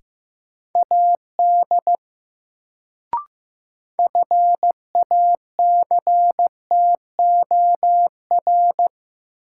Morse code, heard as a keyed tone, about 15 wpm, 700 Hz. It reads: AD FACTOR
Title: Sentences from Top 1300 Words - New Word Drill - 15wpm